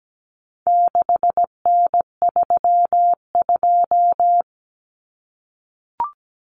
Morse code, heard as a keyed tone, about 17 wpm, 700 Hz. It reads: 6N32
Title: Common Cut Numbers - 4-Digit 17wpm